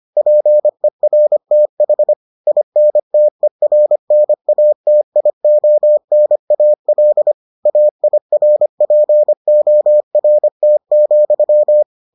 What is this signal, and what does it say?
Morse code, heard as a keyed tone, about 25 wpm, 600 Hz. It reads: PERTH INTERNATIONAL AIRPORT,